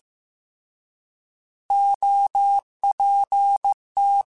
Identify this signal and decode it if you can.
Morse code, heard as a keyed tone, about 15 wpm, 800 Hz. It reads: OPT